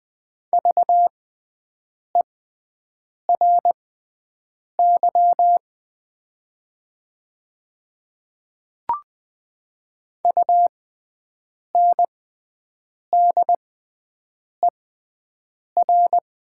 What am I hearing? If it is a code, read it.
Morse code, heard as a keyed tone, about 20 wpm, spaced as for 7 wpm, 700 Hz. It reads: VERY UNDER
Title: 6x Character Spacing - Top 200 Words - 20wpm